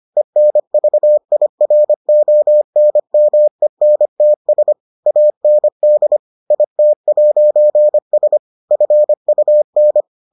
Morse code, heard as a keyed tone, about 25 wpm, 600 Hz. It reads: ENVIRONMENTS AND IT'S FUN